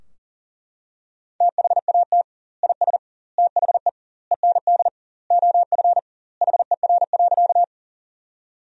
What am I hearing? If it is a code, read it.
Morse code, heard as a keyed tone, about 40 wpm, 700 Hz. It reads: THAT IS THE END OF HER.
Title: 2x Word-Spacing - Sentences from Top 200 Words - 40wpm